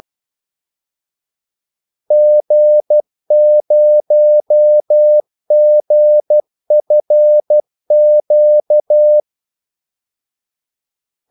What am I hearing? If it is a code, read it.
Morse code, heard as a keyed tone, about 12 wpm, 600 Hz. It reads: G0GFQ
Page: 2 ekstra ordmellemrum mellem kaldesignalerne